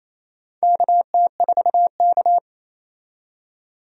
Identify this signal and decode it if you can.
Morse code, heard as a keyed tone, about 27 wpm, 700 Hz. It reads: KT4K
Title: USA Call Signs - 4 Characters - 28wpm